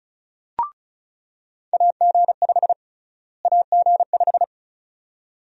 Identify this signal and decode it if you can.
Morse code, heard as a keyed tone, about 36 wpm, 700 Hz. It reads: AG5 AG5
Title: USA Call Sign Prefixes - Repeated 2x - 35wpm